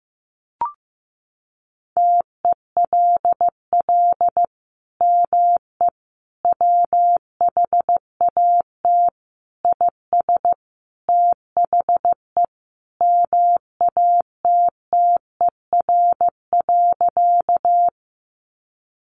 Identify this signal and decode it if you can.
Morse code, heard as a keyed tone, about 14 wpm, 700 Hz. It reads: TELL ME WHAT IS THE MATTER.
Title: Sentences from Top 700 Words 15wpm